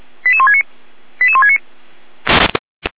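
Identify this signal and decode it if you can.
Five-tone selective call (ZVEI), heard as quick runs of five tones, tones of about 70 ms. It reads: 88169 88169
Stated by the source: ZVEI five-tone call received on an SDR.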